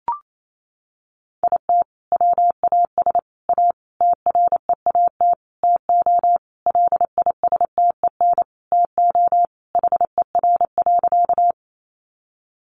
Morse code, heard as a keyed tone, about 28 wpm, 700 Hz. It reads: IT WAS A TREAT TO LISTEN TO HER.
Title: Sentences from Top 1300 Words 28wpm